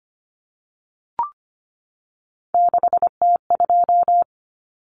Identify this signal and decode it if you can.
Morse code, heard as a keyed tone, about 25 wpm, 700 Hz. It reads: 6T2
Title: Common Cut Numbers - 3-Digit 25wpm